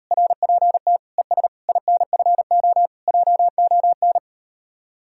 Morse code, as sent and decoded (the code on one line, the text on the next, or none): .-. .--. - / . ... / .. -. ..-. --- / .--- --- -.
RPT ES INFO JON